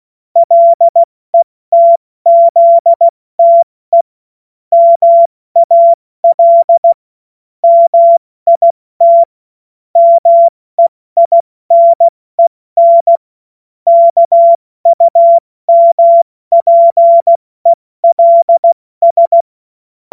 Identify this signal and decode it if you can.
Morse code, heard as a keyed tone, about 16 wpm, 680 Hz. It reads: LETZTE MAL MIT MEINEN KUMPELS
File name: Morsekurs / Witze / Witz 19. Oktober 2021, 70 F80